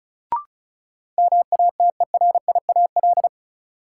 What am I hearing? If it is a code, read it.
Morse code, heard as a keyed tone, about 36 wpm, 700 Hz. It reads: MATERIAL
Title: Sentences from Top 1000 Words - Review Part 5 - New Word Drill - 35wpm